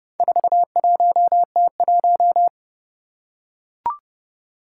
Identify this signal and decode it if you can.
Morse code, heard as a keyed tone, about 29 wpm, 700 Hz. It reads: 41T1
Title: Four Challenging Letter-Number Combinations 30wpm